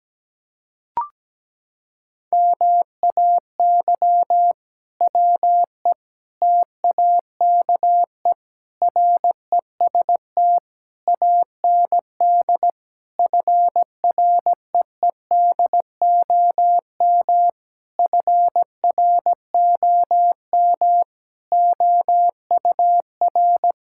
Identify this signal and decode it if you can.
Morse code, heard as a keyed tone, about 17 wpm, 700 Hz. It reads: MAY WE TAKE REST AND FREEDOM FROM OUR
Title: Sentences from Top 1600 Words 17wpm